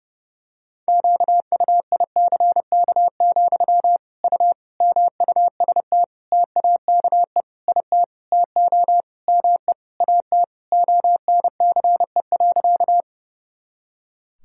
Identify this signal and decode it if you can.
Morse code, heard as a keyed tone, about 30 wpm, 700 Hz. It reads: QUICK, U MUST TAKE IT TO ME AT ONCE.
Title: Sentences from Top 500 Words and Abbreviations 30wpm